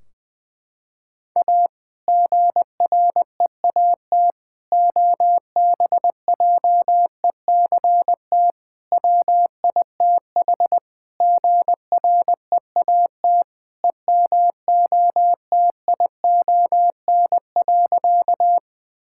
Morse code, heard as a keyed tone, about 20 wpm, 700 Hz. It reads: A GREAT OBJECT WITH GREAT EMOTION.
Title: Sentences from Top 1600 Words 20wpm